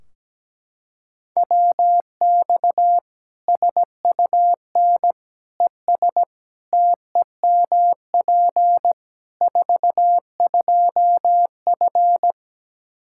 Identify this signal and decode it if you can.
Morse code, heard as a keyed tone, about 17 wpm, 700 Hz. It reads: WX SUN ES TEMP 42F
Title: K1USN SST QSO Practice 17wpm